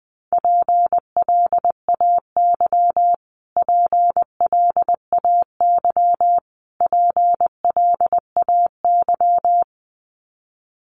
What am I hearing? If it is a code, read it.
Morse code, heard as a keyed tone, about 20 wpm, 700 Hz. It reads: PLAY PLAY PLAY
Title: Top 500 Words Repeated 20wpm